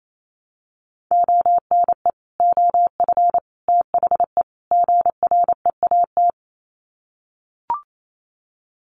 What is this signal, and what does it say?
Morse code, heard as a keyed tone, about 28 wpm, 700 Hz. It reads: ONE OF THE GREAT
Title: Sets of 4 Words 28wpm